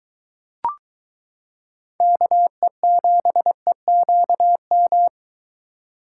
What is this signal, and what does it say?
Morse code, heard as a keyed tone, about 22 wpm, 700 Hz. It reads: KE7EQM